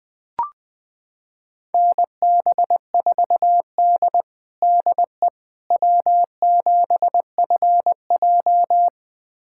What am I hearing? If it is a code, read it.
Morse code, heard as a keyed tone, about 20 wpm, 700 Hz. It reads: NB4D DE W7FJ
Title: USA Callsign DE Callsign - 4 Characters - 20wpm